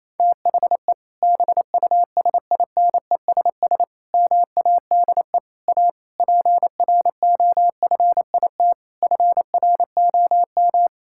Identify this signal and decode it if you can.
Morse code, heard as a keyed tone, about 28 wpm, 700 Hz. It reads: THE BUSINESS MADE A PROFIT FROM
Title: Sentences from Top 1000 Words - Review Part 5 - 28wpm